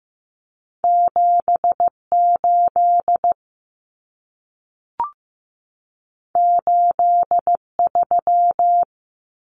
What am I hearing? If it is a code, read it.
Morse code, heard as a keyed tone, about 15 wpm, 700 Hz. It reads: 78 83
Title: Numbers - 2-Digit 15wpm